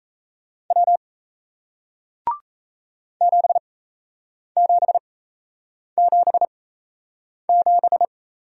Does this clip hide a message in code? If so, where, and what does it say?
Yes: Morse code from 0.7 to 8.1 s, heard as a keyed tone, about 38 wpm, spaced as for 8 wpm, 700 Hz. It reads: W 7777